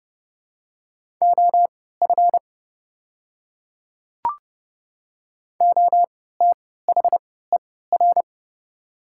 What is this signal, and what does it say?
Morse code, heard as a keyed tone, about 30 wpm, spaced as for 17 wpm, 700 Hz. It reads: OF OTHER
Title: 3x Character Spacing - Top 100 Words - 30wpm